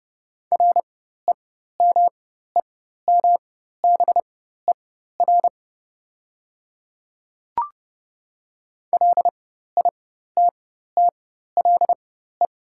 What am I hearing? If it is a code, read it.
Morse code, heard as a keyed tone, about 30 wpm, spaced as for 14 wpm, 700 Hz. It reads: REMEMBER LITTLE
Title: 4x Character Spacing - Top 500 Words - 30wpm